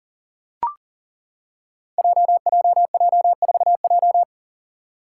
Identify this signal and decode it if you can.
Morse code, heard as a keyed tone, about 41 wpm, 700 Hz. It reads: JJJ4J
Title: Five Challenging Letter-Number Combinations 40wpm